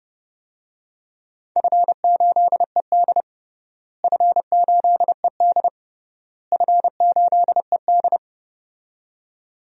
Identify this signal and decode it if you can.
Morse code, heard as a keyed tone, about 30 wpm, 700 Hz. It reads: F8ED F8ED F8ED
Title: DXCC-100 Call Signs - 4 Characters - Repeated 3x - 30wpm